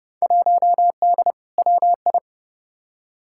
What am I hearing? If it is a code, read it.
Morse code, heard as a keyed tone, about 30 wpm, 700 Hz. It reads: 1D WI
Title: Field Day Exchanges - 30wpm